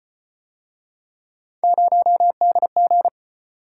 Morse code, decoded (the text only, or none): 0DG